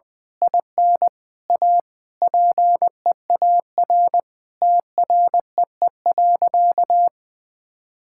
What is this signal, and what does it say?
Morse code, heard as a keyed tone, about 19 wpm, 700 Hz. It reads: IN A PEAR TREE.
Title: Sentences with 9-Word Phrases - 20wpm